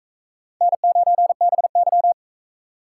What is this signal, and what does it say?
Morse code, heard as a keyed tone, about 44 wpm, 700 Hz. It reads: N9BY